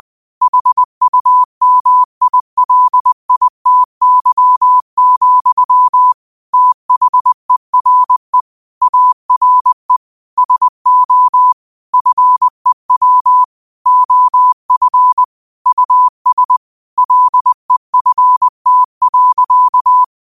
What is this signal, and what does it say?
Morse code, heard as a keyed tone, about 20 wpm, 1000 Hz. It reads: HUMILITY, THERE ARE SO FEW OF US LEFT.